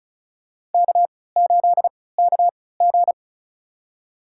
Morse code, heard as a keyed tone, about 36 wpm, spaced as for 20 wpm, 690 Hz. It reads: K8KG
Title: USA Call Signs - 4 Characters - 3x Character Spacing - 35wpm